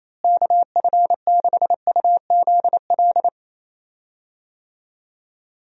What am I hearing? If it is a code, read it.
Morse code, heard as a keyed tone, about 28 wpm, 700 Hz. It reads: KF6UZL
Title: USA Call Signs - 6 Characters - Repeated 2x - 28wpm